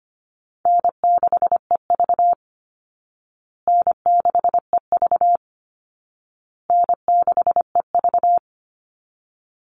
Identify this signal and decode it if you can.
Morse code, heard as a keyed tone, about 25 wpm, 700 Hz. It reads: N6EV N6EV N6EV
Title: Call Signs Repeated 2x - 25wpm